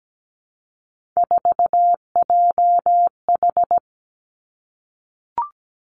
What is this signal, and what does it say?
Morse code, heard as a keyed tone, about 17 wpm, 700 Hz. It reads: 4JH
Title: Three Challenging Letter-Number Combinations 17wpm